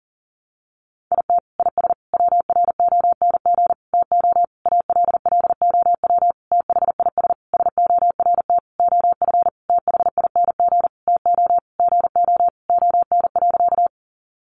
Morse code, heard as a keyed tone, about 37 wpm, 700 Hz. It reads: IT IS WRONG TO ALLOW THIS SORT OF THING TO GO ON.
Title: Sentences from Top 1300 Words 40wpm